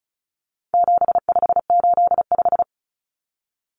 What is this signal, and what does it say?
Morse code, heard as a keyed tone, about 34 wpm, 700 Hz. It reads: 7585
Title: Common Cut Numbers - 4-Digit 35wpm